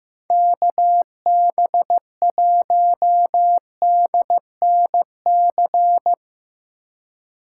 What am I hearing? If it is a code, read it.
Morse code, heard as a keyed tone, about 15 wpm, 700 Hz. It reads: KB1DNC
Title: USA Call Signs - 6 Characters - 4x Character Spacing - 15wpm